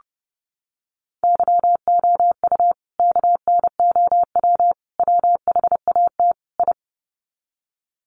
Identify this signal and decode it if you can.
Morse code, heard as a keyed tone, about 30 wpm, 700 Hz. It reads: YOU KNOW WHAT I
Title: Sets of 4 Words 30wpm